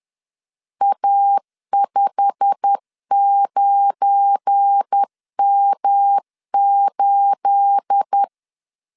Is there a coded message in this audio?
Yes: Morse code from 0.8 to 8.2 s, heard as a keyed tone, about 10 wpm, 800 Hz. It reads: A59M8